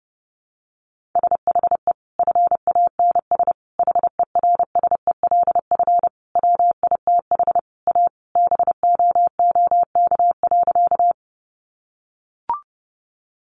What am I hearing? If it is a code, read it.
Morse code, heard as a keyed tone, about 30 wpm, 700 Hz. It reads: SHE FANS HERSELF WITH A BOOK.
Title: Easy Phrases 30wpm